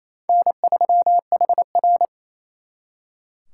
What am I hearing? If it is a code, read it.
Morse code, heard as a keyed tone, about 34 wpm, 700 Hz. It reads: N3HR
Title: Speed Racing - USA 1x2 Call Signs 28wpm